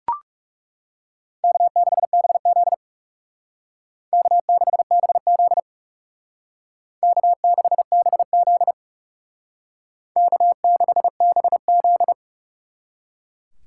Morse code, heard as a keyed tone, about 37 wpm, 700 Hz. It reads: K6BZ K6BZ K6BZ K6BZ
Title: Speed Racing - USA 1x2 Call Signs 30wpm